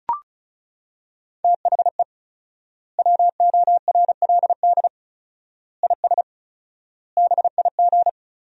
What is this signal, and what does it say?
Morse code, heard as a keyed tone, about 36 wpm, 700 Hz. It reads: THE WORLD IS BIG
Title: 4x Word-Spacing - Sentences from Top 400 Words - 35wpm